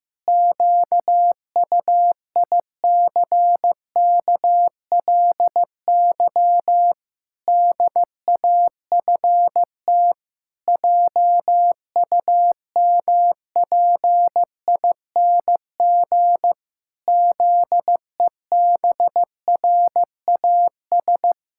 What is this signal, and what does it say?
Morse code, heard as a keyed tone, about 15 wpm, 700 Hz. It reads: QUICKLY DAFT JUMPING ZEBRAS
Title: Pangram Challenge v2 - 15wpm